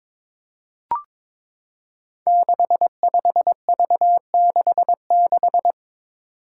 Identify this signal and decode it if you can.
Morse code, heard as a keyed tone, about 22 wpm, 700 Hz. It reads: 65V66
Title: Five Challenging Letter-Number Combinations 22wpm